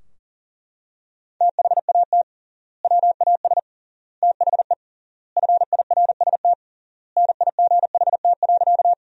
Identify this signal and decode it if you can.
Morse code, heard as a keyed tone, about 40 wpm, 700 Hz. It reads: THAT WAS THE FIRST NIGHT.
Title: 3x Word-Spacing - Sentences from Top 300 Words - 40wpm